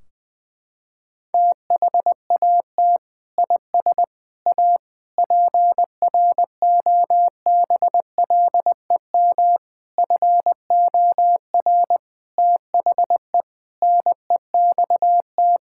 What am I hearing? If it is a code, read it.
Morse code, heard as a keyed tone, about 21 wpm, 700 Hz. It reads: THAT IS A PROBLEM FOR THE NEXT
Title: Sentences from Top 1600 Words 20wpm